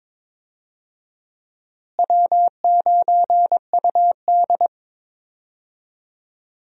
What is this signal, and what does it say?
Morse code, heard as a keyed tone, about 22 wpm, 700 Hz. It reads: W9UD
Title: USA Call Signs - 4 Characters - Repeated 3x - 22wpm